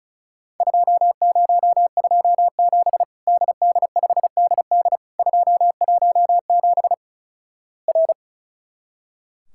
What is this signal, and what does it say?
Morse code, heard as a keyed tone, about 36 wpm, 700 Hz. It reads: 2027 DD5DD 217 R
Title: Worked All Europe DX Contest - QTC Exchange - 35wpm